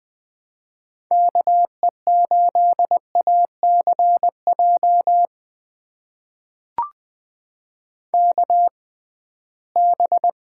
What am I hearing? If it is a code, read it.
Morse code, heard as a keyed tone, about 20 wpm, spaced as for 15 wpm, 700 Hz. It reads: KE8ACJ KB
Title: USA Call Signs - 6 Characters - 6x Character Spacing - 20wpm